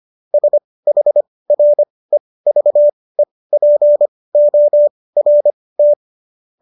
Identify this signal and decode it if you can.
Morse code, heard as a keyed tone, about 25 wpm, spaced as for 18 wpm, 590 Hz. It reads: SHREVEPORT
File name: qso211 18